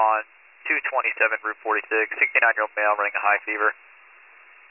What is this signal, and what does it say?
Emergency Alert System (SAME) tones, heard as harsh buzzing bursts, with speech. It is heard from 1.1 to 3.8 s.